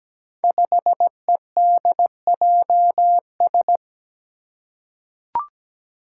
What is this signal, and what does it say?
Morse code, heard as a keyed tone, about 17 wpm, 700 Hz. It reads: 5EDJS
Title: Five Challenging Letter-Number Combinations 17wpm